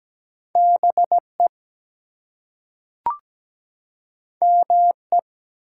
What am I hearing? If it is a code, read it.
Morse code, heard as a keyed tone, about 17 wpm, 700 Hz. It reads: BE ME
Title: Two Letter Words - 17wpm